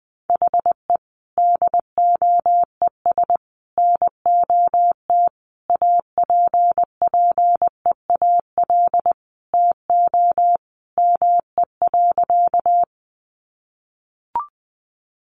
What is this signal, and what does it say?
Morse code, heard as a keyed tone, about 20 wpm, 700 Hz. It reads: HE DOES NOT APPEAL TO ME.
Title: Sentences from Top 1300 Words 20wpm